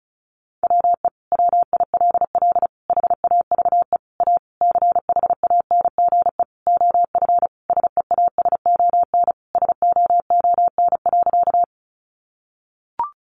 Morse code, heard as a keyed tone, about 34 wpm, 700 Hz. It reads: WE WILL HAVE A CHANGE OF SEASON SOON.
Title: Sentences from Top 1400 Words 35wpm